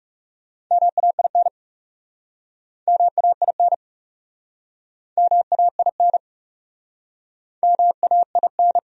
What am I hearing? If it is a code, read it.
Morse code, heard as a keyed tone, about 40 wpm, 700 Hz. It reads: MAIN MAIN MAIN MAIN